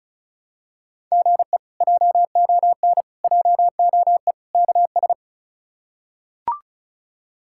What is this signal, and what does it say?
Morse code, heard as a keyed tone, about 34 wpm, 700 Hz. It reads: GE JON JOE KS
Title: K1USN SST QSO Practice - 35wpm